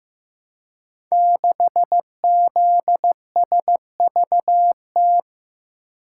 Morse code, heard as a keyed tone, about 15 wpm, 700 Hz. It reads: 6ZSVT
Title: Code Groups - 5 Letter:Numbers - 15wpm